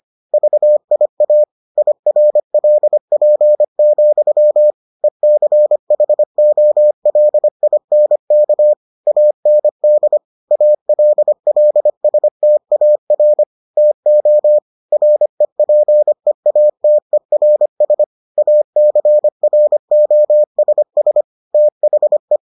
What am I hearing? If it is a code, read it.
Morse code, heard as a keyed tone, about 25 wpm, 600 Hz. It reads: VIA IRLP, ECHOLINK AND ALLSTAR TO REPEATERS ACROSS THE